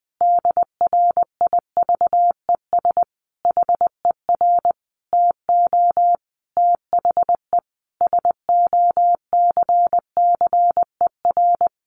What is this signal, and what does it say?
Morse code, heard as a keyed tone, about 20 wpm, 700 Hz. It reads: DRIVES HER TO THE SOCCER
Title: Easy Phrases 20wpm